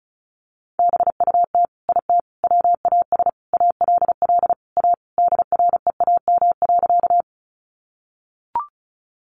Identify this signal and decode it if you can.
Morse code, heard as a keyed tone, about 34 wpm, 700 Hz. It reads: BUT IT WAS ALL A DREAM.
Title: Sentences from Top 1500 Words 35wpm